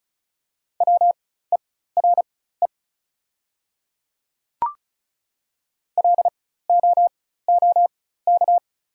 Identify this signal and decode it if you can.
Morse code, heard as a keyed tone, about 36 wpm, spaced as for 16 wpm, 690 Hz. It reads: WERE LOOK
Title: 4x Character Spacing - Top 100 Words - 35wpm